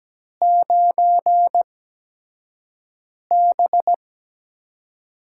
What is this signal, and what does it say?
Morse code, heard as a keyed tone, about 17 wpm, spaced as for 5 wpm, 700 Hz. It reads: 9B
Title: USA Call Signs - 6 Characters - 8x Character Spacing - 17wpm